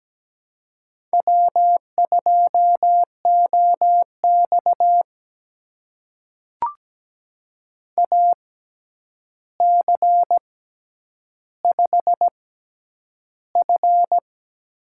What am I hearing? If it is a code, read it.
Morse code, heard as a keyed tone, about 17 wpm, spaced as for 9 wpm, 700 Hz. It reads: W2OX AC5F
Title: USA Call Signs - 4 Characters - 6x Character Spacing - 17wpm